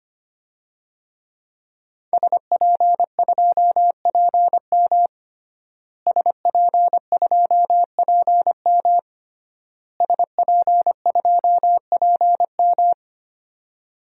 Morse code, heard as a keyed tone, about 25 wpm, 700 Hz. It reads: SP2PM SP2PM SP2PM
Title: DXCC-100 Call Signs - 5 Characters - Repeated 3x - 25wpm